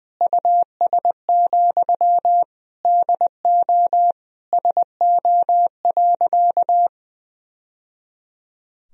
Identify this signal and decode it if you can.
Morse code, heard as a keyed tone, about 20 wpm, 700 Hz. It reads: US, DO SO.
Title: Sentences from Top 1000 Words - Review Part 2 - 20wpm